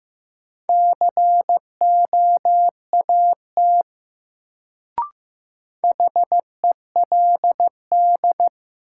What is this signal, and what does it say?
Morse code, heard as a keyed tone, about 16 wpm, 700 Hz. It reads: COAT HELD
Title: Sentences from Top 1600 Words - New Word Drill Alt - 15wpm